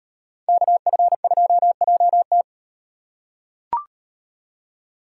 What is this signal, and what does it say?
Morse code, heard as a keyed tone, about 38 wpm, 700 Hz. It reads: KF2JT